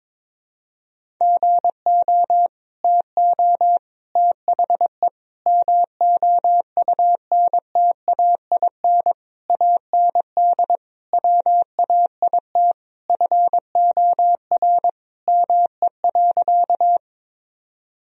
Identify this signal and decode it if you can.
Morse code, heard as a keyed tone, about 22 wpm, 700 Hz. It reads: GO TO THE MOUNTAIN AND WAIT FOR ME.
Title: Sentences from Top 1000 Words - Review Part 1 - 22wpm